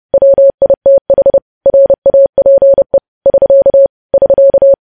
Morse code, heard as a keyed tone, about 30 wpm, 550 Hz. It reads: WITH RAPE <SK> <SK>
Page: news headlines in morse code at 30 wpm